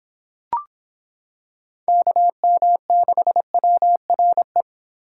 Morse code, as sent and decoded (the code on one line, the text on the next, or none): -.- -- -.... .-- .-. .
KM6WRE